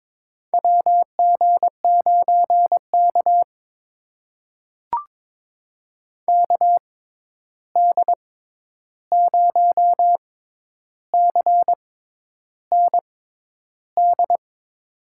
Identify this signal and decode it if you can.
Morse code, heard as a keyed tone, about 22 wpm, spaced as for 10 wpm, 700 Hz. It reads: WG9K KD0CND
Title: USA Call Signs - 6x Character Spacing - 22wpm